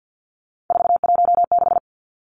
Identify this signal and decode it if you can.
Morse code, heard as a keyed tone, about 48 wpm, 700 Hz. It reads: 416
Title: Numbers - 3-Digit 50wpm